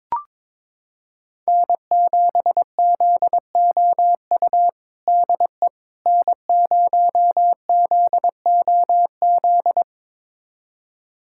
Morse code, heard as a keyed tone, about 22 wpm, 700 Hz. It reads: N7ZOU DE N0ZOZ
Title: USA Callsign DE Callsign - 5 Characters - 22wpm